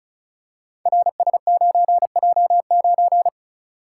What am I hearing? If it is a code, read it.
Morse code, heard as a keyed tone, about 36 wpm, 690 Hz. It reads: RS9J9
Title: Code Groups - 5 Letter:Numbers - 35wpm